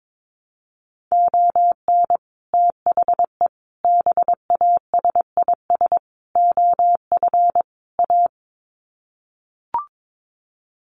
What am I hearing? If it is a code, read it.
Morse code, heard as a keyed tone, about 22 wpm, 700 Hz. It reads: ON THE BASIS OF A
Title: Sets of 5 Words Encore 22wpm